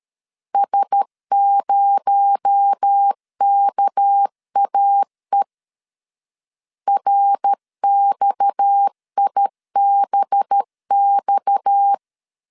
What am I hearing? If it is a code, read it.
Morse code, heard as a keyed tone, about 13 wpm, 800 Hz. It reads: S0KAE RXIBX
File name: CW / DET / 1 6 1500C S B 12wpm 3431